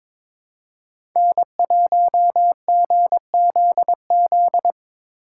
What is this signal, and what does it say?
Morse code, heard as a keyed tone, about 22 wpm, 700 Hz. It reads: N1GZZ
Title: USA Call Signs - 5 Characters - 2x Character Spacing - 22wpm